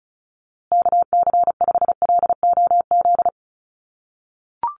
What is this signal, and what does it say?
Morse code, heard as a keyed tone, about 34 wpm, 700 Hz. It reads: KC5LOZ
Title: USA Call Signs Repeated 2x 35wpm